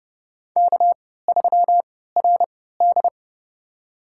Morse code, heard as a keyed tone, about 30 wpm, spaced as for 17 wpm, 700 Hz. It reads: K3RD
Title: USA Call Signs - 3x Character Spacing - 30wpm